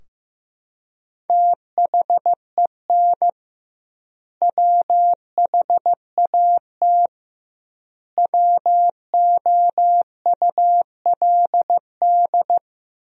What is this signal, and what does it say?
Morse code, heard as a keyed tone, about 16 wpm, 700 Hz. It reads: THEN WHAT WOULD
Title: 2x Word-Spacing - Sentences from Top 400 Words - 15wpm